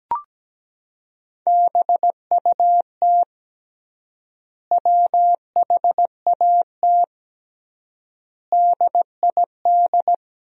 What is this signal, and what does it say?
Morse code, heard as a keyed tone, about 17 wpm, 700 Hz. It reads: BUT WHAT DID
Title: 3x Word-Spacing - Sentences from Top 300 Words - 17wpm